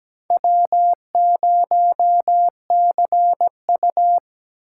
Morse code, heard as a keyed tone, about 17 wpm, 700 Hz. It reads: W0CU
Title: USA Call Signs - 4 Characters - 3x Character Spacing - 17wpm